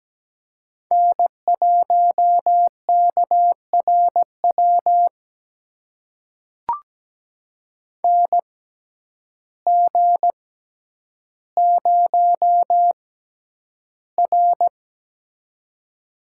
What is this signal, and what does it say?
Morse code, heard as a keyed tone, about 17 wpm, spaced as for 9 wpm, 700 Hz. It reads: N1KRW NG0R
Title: USA Call Signs - 6x Character Spacing - 17wpm